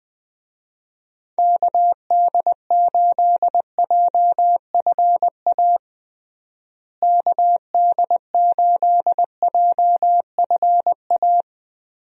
Morse code, heard as a keyed tone, about 21 wpm, 700 Hz. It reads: KD8JFA KD8JFA
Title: USA Call Signs - 6 Characters - Repeated 3x - 20wpm